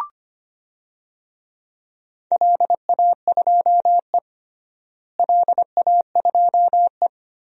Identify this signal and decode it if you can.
Morse code, heard as a keyed tone, about 25 wpm, 700 Hz. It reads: LA2E LA2E
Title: DXCC-100 Call Signs - 4 Characters - Repeated 3x - 25wpm